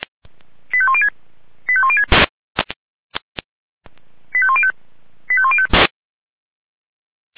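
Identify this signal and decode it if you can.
Five-tone selective call (ZVEI), heard as quick runs of five tones, tones of about 70 ms. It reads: 85106 85106 85105 85105
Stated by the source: Five-tone ZVEI selective call received by SDR.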